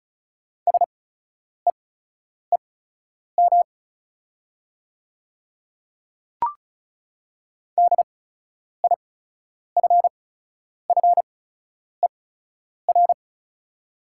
Morse code, heard as a keyed tone, about 34 wpm, spaced as for 10 wpm, 690 Hz. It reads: SEEM DIFFER
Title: 8x Character Spacing - Top 500 Words - 35wpm